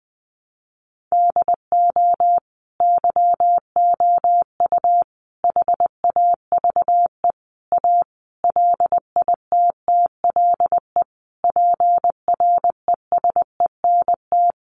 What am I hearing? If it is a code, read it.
Morse code, heard as a keyed tone, about 20 wpm, 700 Hz. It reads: DO YOU HAVE A LITTLE PRESENT
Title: Sentences from Top 800 Words 20wpm